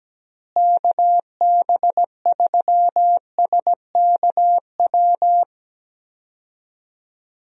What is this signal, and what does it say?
Morse code, heard as a keyed tone, about 17 wpm, 700 Hz. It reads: KB3SKW
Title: USA Call Signs - 6 Characters - Repeated 3x - 17wpm